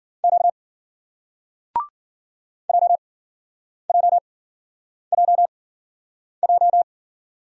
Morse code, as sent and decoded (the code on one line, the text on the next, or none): --.- / .--- .--- .--- .---
Q JJJJ